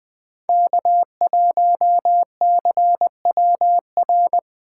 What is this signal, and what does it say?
Morse code, heard as a keyed tone, about 23 wpm, 700 Hz. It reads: K1CWR
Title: Speed Racing - USA Call Signs 20wpm